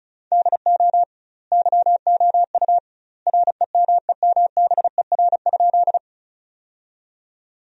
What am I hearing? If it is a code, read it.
Morse code, heard as a keyed tone, about 36 wpm, 700 Hz. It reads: DO YOU REMEMBER?
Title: 2x Word-Spacing - Sentences from Top 400 Words - 35wpm